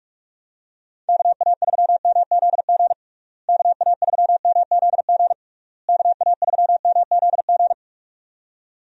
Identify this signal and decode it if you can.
Morse code, heard as a keyed tone, about 41 wpm, 700 Hz. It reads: KA3MZG KA3MZG KA3MZG
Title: USA Call Signs - 6 Characters - Repeated 3x - 45wpm